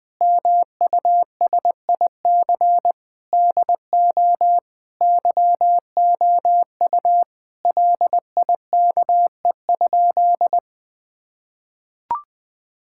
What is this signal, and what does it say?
Morse code, heard as a keyed tone, about 20 wpm, 700 Hz. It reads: MUSIC DO YOU LIKE?
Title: Sentences with 3-Word Phrases - 20wpm